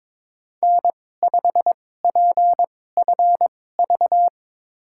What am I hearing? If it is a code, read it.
Morse code, heard as a keyed tone, about 22 wpm, spaced as for 16 wpm, 700 Hz. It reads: N5PFV
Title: USA Call Signs - 2x Character Spacing - 22wpm